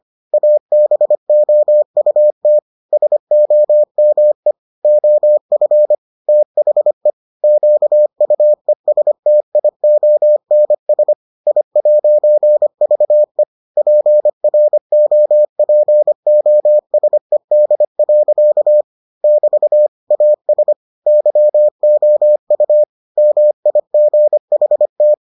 Morse code, heard as a keyed tone, about 25 wpm, 600 Hz. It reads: ABOUT SOME OF THE QUESTIONS I'VE PROPOSED. <BT> AS YOU MIGHT